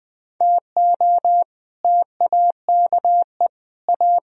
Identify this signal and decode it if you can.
Morse code, heard as a keyed tone, about 20 wpm, 700 Hz. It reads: TO TAKE A
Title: Speed-Racing - Sets of 4 Words 20wpm